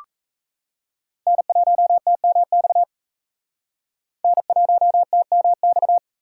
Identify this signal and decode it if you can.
Morse code, heard as a keyed tone, about 35 wpm, 700 Hz. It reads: N1TMX N1TMX
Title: Speed Racing - USA 1x3 Call Signs 28wpm